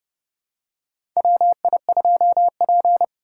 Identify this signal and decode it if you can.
Morse code, heard as a keyed tone, about 30 wpm, 700 Hz. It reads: WI2P
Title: USA Call Signs - 4 Characters - 2x Character Spacing - 30wpm